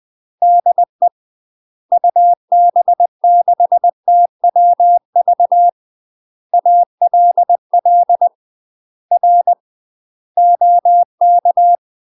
Morse code, heard as a keyed tone, about 20 wpm, 700 Hz. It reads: DE UB6TWV ALL R OK
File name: UK MASTER QSOs TRAINING / 20WPM / UK MASTER QSOs 0045